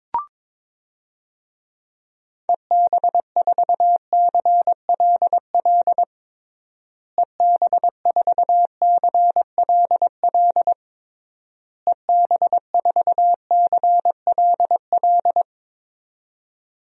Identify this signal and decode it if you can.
Morse code, heard as a keyed tone, about 22 wpm, 700 Hz. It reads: EB4CLL EB4CLL EB4CLL
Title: DXCC-100 Call Signs - 6 Characters - Repeated 3x - 22wpm